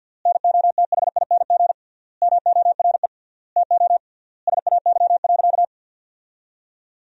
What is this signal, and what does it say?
Morse code, heard as a keyed tone, about 50 wpm, 700 Hz. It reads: NOTHING MORE TO SAY.
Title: 3x Word-Spacing - Sentences from Top 500 Words - 50wpm